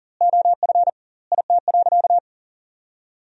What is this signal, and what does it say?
Morse code, heard as a keyed tone, about 40 wpm, 700 Hz. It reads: OF IT.
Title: 2x Word-Spacing - Sentences from Top 400 Words - 40wpm